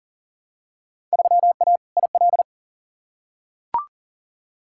Morse code, heard as a keyed tone, about 40 wpm, 700 Hz. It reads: 3A IL
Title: Field Day Exchanges - 40wpm